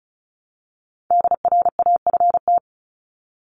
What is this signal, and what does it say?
Morse code, heard as a keyed tone, about 34 wpm, 700 Hz. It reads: DRAFT
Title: Top 1751-2000 Words 35wpm